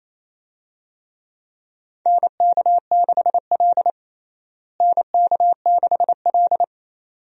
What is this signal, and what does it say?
Morse code, heard as a keyed tone, about 28 wpm, 700 Hz. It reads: NK6L NK6L
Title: USA Call Signs - 4 Characters - Repeated 2x - 28wpm